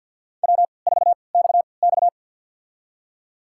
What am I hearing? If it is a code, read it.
Morse code, heard as a keyed tone, about 51 wpm, spaced as for 29 wpm, 700 Hz. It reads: W4XX
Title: USA Call Signs - 3x Character Spacing - 50wpm